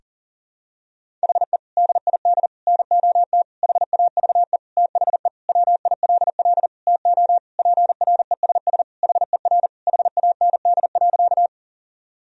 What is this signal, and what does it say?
Morse code, heard as a keyed tone, about 40 wpm, 700 Hz. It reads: HE DID NOT HAVE THE WILL TO PRESS HER HAND.
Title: Sentences from Top 1000 Words - Review Part 1 - 40wpm